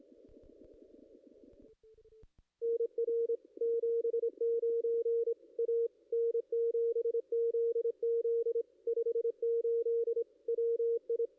Morse code, heard as a keyed tone, about 25 wpm, 460 Hz. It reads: NR 79 A N7ZZ 58 WI
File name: contests / ss-cw-k5tr-op-n5rz -2300